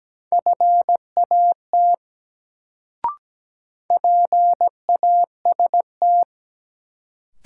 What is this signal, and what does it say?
Morse code, heard as a keyed tone, about 17 wpm, 700 Hz. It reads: FAT PAST
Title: Sentences from Top 600 Words - New Word Drill - 17wpm